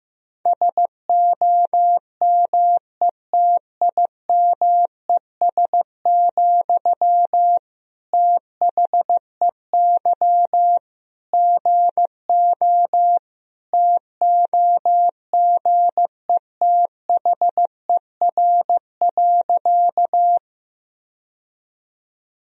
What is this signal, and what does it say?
Morse code, heard as a keyed tone, about 15 wpm, 700 Hz. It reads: SOMETIMES, THEY GO TOGETHER.
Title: Sentences from Top 1000 Words - Review Part 3 - 15wpm